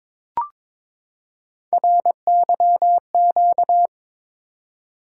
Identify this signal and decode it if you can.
Morse code, heard as a keyed tone, about 22 wpm, 700 Hz. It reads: RYQ
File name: practice / 22wpm / Three Letter-Number Combinations - 22wpm